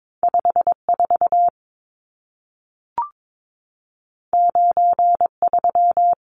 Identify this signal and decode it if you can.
Morse code, heard as a keyed tone, about 22 wpm, 700 Hz. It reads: 54 93
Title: Numbers - 2-Digit 22wpm